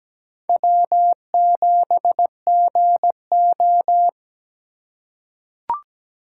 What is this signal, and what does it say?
Morse code, heard as a keyed tone, about 17 wpm, 700 Hz. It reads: W7GO
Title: USA Call Signs - 4 Characters - 3x Character Spacing - 17wpm